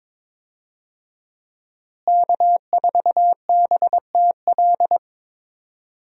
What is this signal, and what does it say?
Morse code, heard as a keyed tone, about 21 wpm, 700 Hz. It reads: K4BTL
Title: USA Call Signs - 5 Characters - Repeated 3x - 22wpm